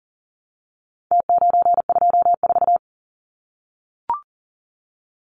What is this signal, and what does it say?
Morse code, heard as a keyed tone, about 40 wpm, 700 Hz. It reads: T924
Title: Numbers - Contest Style - 4-Digit 40wpm